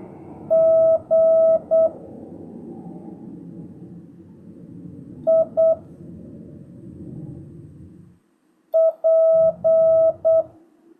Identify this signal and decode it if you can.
Morse code, heard as a keyed tone, about 8 wpm, 650 Hz. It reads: G I P
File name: jh8cbh / 2025cw